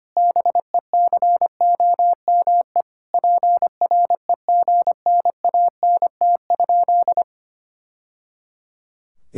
Morse code, heard as a keyed tone, about 25 wpm, 700 Hz. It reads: BECOME PREGNANT?
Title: Sentences from Top 1000 Words - Review Part 6 - 25wpm